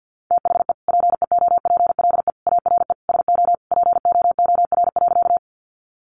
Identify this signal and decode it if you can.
Morse code, heard as a keyed tone, about 50 wpm, 700 Hz. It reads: THE PEOPLE ARE SO POOR.
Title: Sentences from Top 1000 Words 50wpm